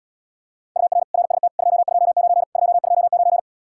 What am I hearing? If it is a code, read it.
Morse code, heard as a keyed tone, about 78 wpm, 700 Hz. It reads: UR RST 229 229